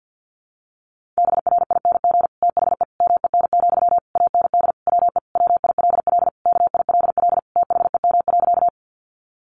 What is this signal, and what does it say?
Morse code, heard as a keyed tone, about 50 wpm, 700 Hz. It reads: BRING THE MEN, AND WE WILL KILL THEM.
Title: Sentences from Top 1400 Words 50wpm